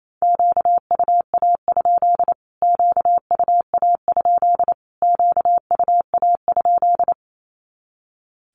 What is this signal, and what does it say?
Morse code, heard as a keyed tone, about 28 wpm, 700 Hz. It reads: QUA? QUA? QUA?
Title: Q-Codes Repeated 3x 28wpm